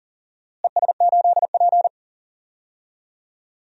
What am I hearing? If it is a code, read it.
Morse code, heard as a keyed tone, about 39 wpm, 700 Hz. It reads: ES8P
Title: DXCC-100 Call Signs - 4 Characters - Repeated 3x - 40wpm